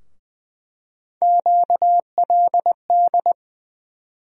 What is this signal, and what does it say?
Morse code, heard as a keyed tone, about 20 wpm, 700 Hz. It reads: QLD
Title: USA Call Sign Postfixes - 3 Letters 20wpm